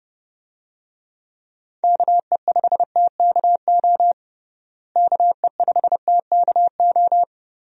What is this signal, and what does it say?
Morse code, heard as a keyed tone, about 30 wpm, 700 Hz. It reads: KE5TKO KE5TKO
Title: USA Call Signs - 6 Characters - Repeated 2x - 30wpm